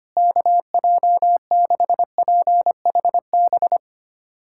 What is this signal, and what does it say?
Morse code, heard as a keyed tone, about 25 wpm, 700 Hz. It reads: KJ6PHB
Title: USA Callsign DE Callsign - 6 Characters - 25wpm